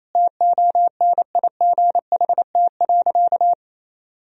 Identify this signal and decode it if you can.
Morse code, heard as a keyed tone, about 28 wpm, 700 Hz. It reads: TONIGHT.
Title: Sentences from Top 1000 Words - Review Part 4 - 28wpm